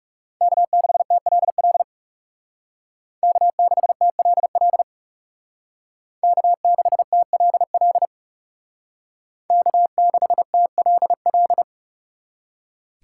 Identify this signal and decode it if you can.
Morse code, heard as a keyed tone, about 36 wpm, 700 Hz. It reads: K6TLL K6TLL K6TLL K6TLL
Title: Speed Racing - USA Call Signs 30wpm